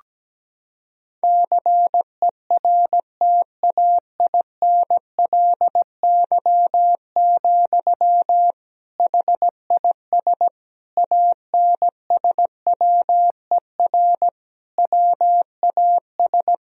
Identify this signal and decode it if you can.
Morse code, heard as a keyed tone, about 17 wpm, 700 Hz. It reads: CERTAINLY, HIS ANSWER WAS
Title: Sentences from Top 1000 Words - Review Part 2 - 17wpm